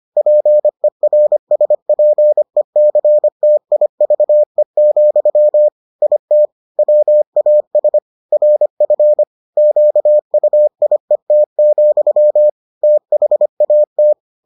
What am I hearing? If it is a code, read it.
Morse code, heard as a keyed tone, about 25 wpm, 600 Hz. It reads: PERSPECTIVE, IT WAS RF QUIET, THAT